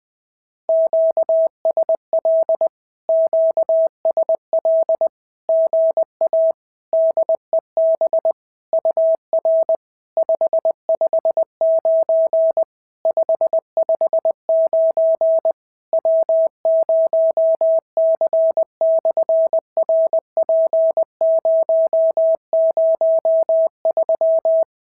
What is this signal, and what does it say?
Morse code, heard as a keyed tone, about 20 wpm, 650 Hz. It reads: QSL QSL GA DEB UR 559 559 W0C/RP003
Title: SOTA QSO Practice 20wpm